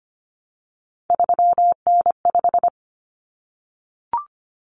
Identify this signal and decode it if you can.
Morse code, heard as a keyed tone, about 25 wpm, 700 Hz. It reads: 3N5
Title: Common Cut Numbers - 3-Digit 25wpm